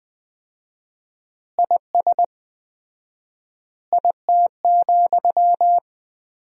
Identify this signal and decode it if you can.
Morse code, heard as a keyed tone, about 20 wpm, 700 Hz. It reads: IS IT,
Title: 4x Word-Spacing - Sentences from Top 200 Words - 20wpm